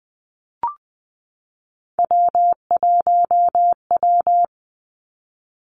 Morse code, heard as a keyed tone, about 20 wpm, 700 Hz. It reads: W1W
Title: Three Challenging Letter-Number Combinations 20wpm